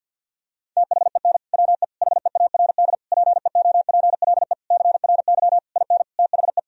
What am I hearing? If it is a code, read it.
Morse code, heard as a keyed tone, about 52 wpm, 700 Hz. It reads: THEN WE HEARD PEOPLE CRY IN THE
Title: Sentences from Top 1000 Words - Review Part 2 - 50wpm